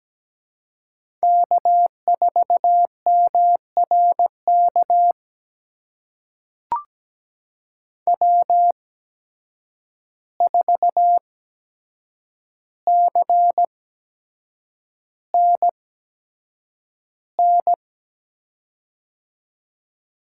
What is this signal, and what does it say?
Morse code, heard as a keyed tone, about 17 wpm, spaced as for 7 wpm, 700 Hz. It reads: K4MRK W4CNN
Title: USA Call Signs - 5 Characters - 8x Character Spacing - 17wpm